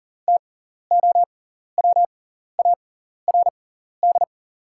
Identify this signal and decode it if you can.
Morse code, heard as a keyed tone, about 40 wpm, spaced as for 14 wpm, 690 Hz. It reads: TOWARD
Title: 6x Character Spacing - Top 500 Words - 40wpm